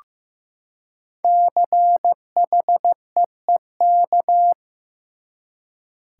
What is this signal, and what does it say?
Morse code, heard as a keyed tone, about 15 wpm, 700 Hz. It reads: CHEEK
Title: Five Letter Words - 15wpm